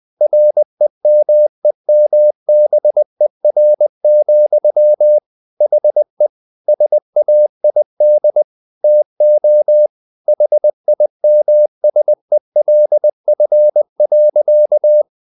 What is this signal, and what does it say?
Morse code, heard as a keyed tone, about 20 wpm, 600 Hz. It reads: REMEMBER, HE SAID TO HIMSELF.
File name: the old man and the sea / the old man and the sea part 1 20wpm